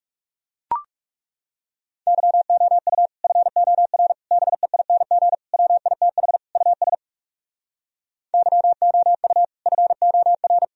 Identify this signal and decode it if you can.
Morse code, heard as a keyed tone, about 40 wpm, 700 Hz. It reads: YOU FOR BEING WITH US YOU FOR